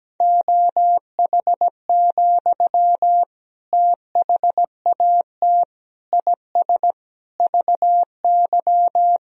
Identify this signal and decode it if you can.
Morse code, heard as a keyed tone, about 17 wpm, 700 Hz. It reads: OH, THAT IS VY
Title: Sentences from Top 500 Words and Abbreviations 17wpm